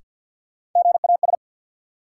Morse code, heard as a keyed tone, about 46 wpm, 700 Hz. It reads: GAS
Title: Sentences from Top 1100 Words - New Word Drill - 50wpm